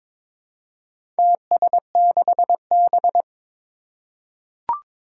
Morse code, heard as a keyed tone, about 22 wpm, 700 Hz. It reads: TS6B
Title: Four Challenging Letter-Number Combinations 22wpm